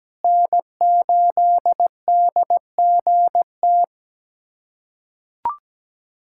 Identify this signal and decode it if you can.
Morse code, heard as a keyed tone, about 17 wpm, 700 Hz. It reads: N8DGT
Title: USA Callsign DE Callsign 17wpm